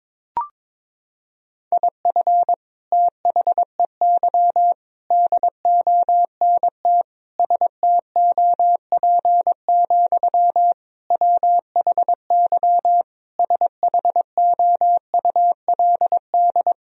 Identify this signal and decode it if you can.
Morse code, heard as a keyed tone, about 21 wpm, 700 Hz. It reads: IF THEY DONT STOP, WHY SHOULD
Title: Sentences from Top 1000 Words - Review Part 1 - 22wpm